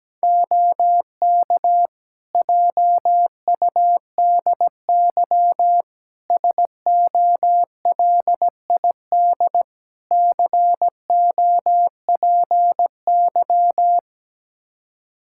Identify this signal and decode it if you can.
Morse code, heard as a keyed tone, about 17 wpm, 700 Hz. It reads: OK JUDY SOLID COPY
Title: Practice QSO - Exchange 3 - More Info - 17wpm